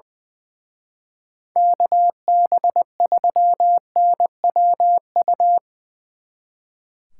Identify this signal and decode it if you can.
Morse code, heard as a keyed tone, about 22 wpm, 700 Hz. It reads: KB3NWU
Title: Speed Racing - USA 2x3 Call Signs 20wpm